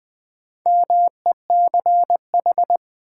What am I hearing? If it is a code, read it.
Morse code, heard as a keyed tone, about 20 wpm, 700 Hz. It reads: MECH
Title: Build A Word - 4 Letters At A Time - 10-Letter Words - 20wpm